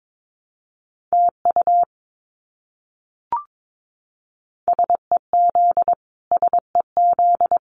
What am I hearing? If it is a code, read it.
Morse code, heard as a keyed tone, about 22 wpm, 700 Hz. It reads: TU SEZ SEZ
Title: Abbreviations Repeated 3x - 22wpm